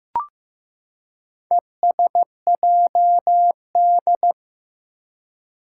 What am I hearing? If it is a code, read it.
Morse code, heard as a keyed tone, about 15 wpm, 700 Hz. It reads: ESJD
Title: Four Challenging Letter-Number Combinations 15wpm